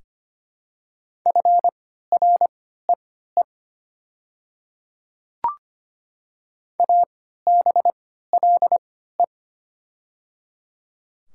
Morse code, heard as a keyed tone, about 25 wpm, spaced as for 14 wpm, 700 Hz. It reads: FREE ABLE
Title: 3x Character Spacing - Top 500 Words - 25wpm